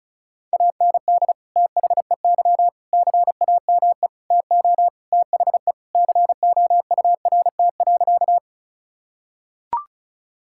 Morse code, heard as a keyed tone, about 36 wpm, 700 Hz. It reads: AND THEY CAME TO THE COURT.
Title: Sentences from Top 1600 Words 35wpm